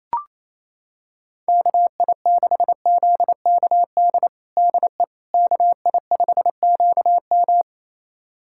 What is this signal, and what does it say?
Morse code, heard as a keyed tone, about 28 wpm, 700 Hz. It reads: KI6ZKD DE KI5QM
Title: USA Callsign DE Callsign 28wpm